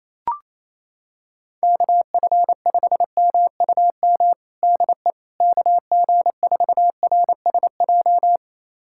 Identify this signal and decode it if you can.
Morse code, heard as a keyed tone, about 28 wpm, 700 Hz. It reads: KF5MUM DE KG4RSJ
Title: USA Callsign DE Callsign - 6 Characters - 28wpm